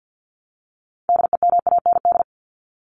Morse code, heard as a keyed tone, about 50 wpm, 700 Hz. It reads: DEMAND
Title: Top 501-750 Words 50wpm